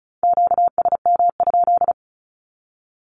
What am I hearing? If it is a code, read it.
Morse code, heard as a keyed tone, about 34 wpm, 700 Hz. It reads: QSM?
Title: Q-Codes 35wpm